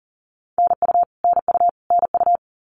Morse code, heard as a keyed tone, about 40 wpm, 700 Hz. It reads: NU NU NU
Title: Canadian Provinces and Territories - Repeated 3x - 40wpm